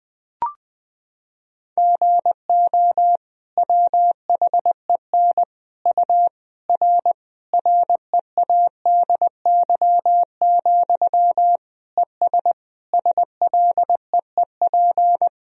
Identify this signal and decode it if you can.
Morse code, heard as a keyed tone, about 20 wpm, 700 Hz. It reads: GO WHEN U R READY, ES SLEEP
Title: Sentences from Top 500 Words and Abbreviations 20wpm